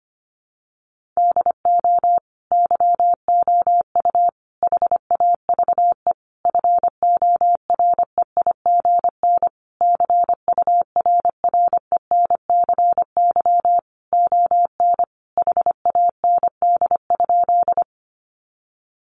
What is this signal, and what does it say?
Morse code, heard as a keyed tone, about 25 wpm, 700 Hz. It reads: DO YOU HAVE FOREIGN CURRENCY ON HAND?
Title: Sentences from Top 1500 Words 25wpm